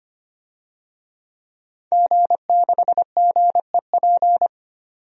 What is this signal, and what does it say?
Morse code, heard as a keyed tone, about 25 wpm, 700 Hz. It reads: G6GEP
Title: DXCC-100 Call Signs - 5 Characters - 25wpm